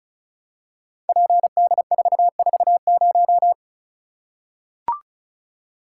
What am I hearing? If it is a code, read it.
Morse code, heard as a keyed tone, about 34 wpm, 700 Hz. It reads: PD440
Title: Code Groups - 5 Letter:Numbers - 35wpm